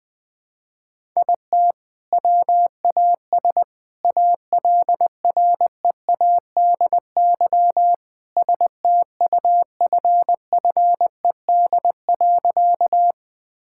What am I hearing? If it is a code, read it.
Morse code, heard as a keyed tone, about 20 wpm, 700 Hz. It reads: IT WAS ALREADY STUFFED.
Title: All-Age Appropriate Jokes - 20wpm